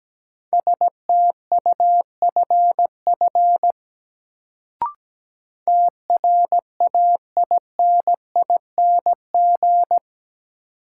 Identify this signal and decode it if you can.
Morse code, heard as a keyed tone, about 17 wpm, 700 Hz. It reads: STUFF TRAINING
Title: Sentences from Top 1400 Words - New Word Drill Alt - 17wpm